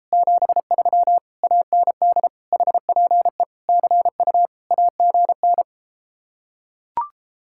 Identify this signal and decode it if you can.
Morse code, heard as a keyed tone, about 33 wpm, 700 Hz. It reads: 73 AND HPE CU AGN